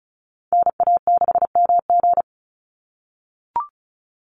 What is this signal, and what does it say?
Morse code, heard as a keyed tone, about 34 wpm, 700 Hz. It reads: NA6MG
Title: NA SOTA Activator Callsigns 35wpm